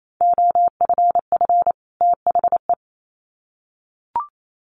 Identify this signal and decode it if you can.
Morse code, heard as a keyed tone, about 28 wpm, 700 Hz. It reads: OFF THE
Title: Sets of 2 Words 28wpm